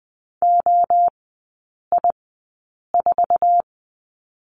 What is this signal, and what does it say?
Morse code, heard as a keyed tone, about 20 wpm, spaced as for 8 wpm, 700 Hz. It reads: OI4
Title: 047 Lesson 9e - Number 4 - Three Characters 20wpm